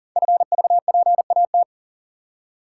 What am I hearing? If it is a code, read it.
Morse code, heard as a keyed tone, about 39 wpm, 700 Hz. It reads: FVPAT
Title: Code Groups - 5 Letters - 40wpm